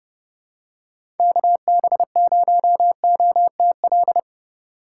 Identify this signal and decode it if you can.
Morse code, heard as a keyed tone, about 30 wpm, 700 Hz. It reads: KB0OTL
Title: USA Call Signs - 4x Character Spacing - 30wpm